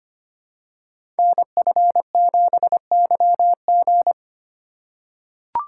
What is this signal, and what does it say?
Morse code, heard as a keyed tone, about 25 wpm, 700 Hz. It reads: NF7YG
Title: Code Groups - 5 Letter:Numbers - 25wpm